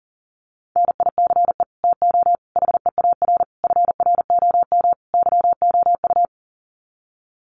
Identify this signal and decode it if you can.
Morse code, heard as a keyed tone, about 37 wpm, 700 Hz. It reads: NICE TO HEAR FROM YOU
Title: Common QSO Phrases 40wpm